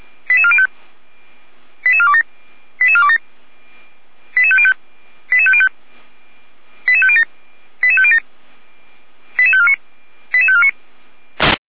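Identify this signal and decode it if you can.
Five-tone selective call (ZVEI), heard as quick runs of five tones, tones of about 70 ms. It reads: 88405 88427 88505 88505 88507 88507 88549 88549
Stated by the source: ZVEI five-tone selective call received by SDR, served by Feuerwehr Orlamünde.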